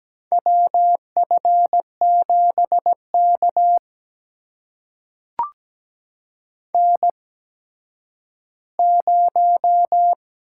Morse code, heard as a keyed tone, about 17 wpm, spaced as for 6 wpm, 700 Hz. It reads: WF7 KN0
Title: USA Call Signs - 8x Character Spacing - 17wpm